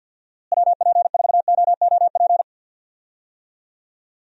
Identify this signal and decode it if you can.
Morse code, heard as a keyed tone, about 50 wpm, 700 Hz. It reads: WP4OOP
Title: USA Call Signs - 6 Characters - Repeated 3x - 50wpm